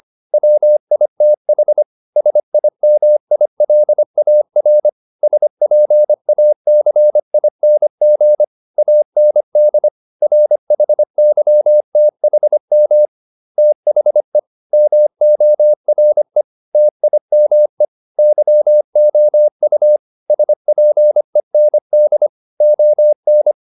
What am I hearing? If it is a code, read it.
Morse code, heard as a keyed tone, about 25 wpm, 600 Hz. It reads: WITH SIMILAR SPACING AND RHYTHM THE MORE TIME YOU SPEND ON